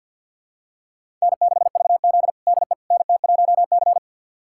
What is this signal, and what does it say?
Morse code, heard as a keyed tone, about 46 wpm, 700 Hz. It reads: N6VZ DE NT1C
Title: USA Callsign DE Callsign - 4 Characters - 50wpm